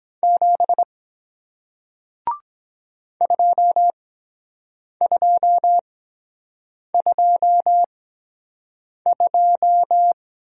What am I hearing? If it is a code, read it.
Morse code, heard as a keyed tone, about 23 wpm, spaced as for 7 wpm, 700 Hz. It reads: 7 2222